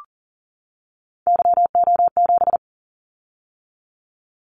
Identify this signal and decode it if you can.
Morse code, heard as a keyed tone, about 38 wpm, 700 Hz. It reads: YO7
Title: Callsign Prefixes for Top 25 DXCC - 40wpm